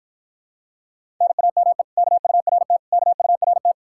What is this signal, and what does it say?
Morse code, heard as a keyed tone, about 54 wpm, 700 Hz. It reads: NAME KURT KURT